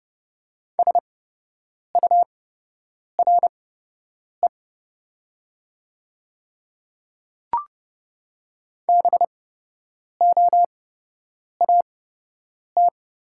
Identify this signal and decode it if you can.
Morse code, heard as a keyed tone, about 30 wpm, spaced as for 8 wpm, 690 Hz. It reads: SURE BOAT
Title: 8x Character Spacing - Top 500 Words - 30wpm